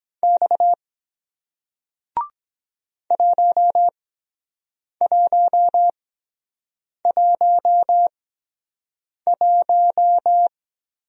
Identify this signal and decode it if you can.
Morse code, heard as a keyed tone, about 23 wpm, spaced as for 7 wpm, 700 Hz. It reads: X 1111